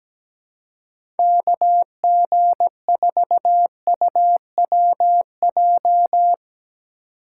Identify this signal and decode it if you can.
Morse code, heard as a keyed tone, about 17 wpm, 700 Hz. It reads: KG4UWJ
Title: USA Call Signs - 6 Characters - 4x Character Spacing - 17wpm